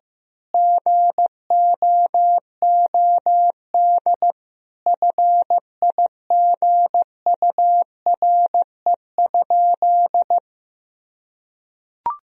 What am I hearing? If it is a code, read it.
Morse code, heard as a keyed tone, about 15 wpm, 700 Hz. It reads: GOOD FIGURE?
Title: Sentences from Top 1000 Words - Review Part 2 - 15wpm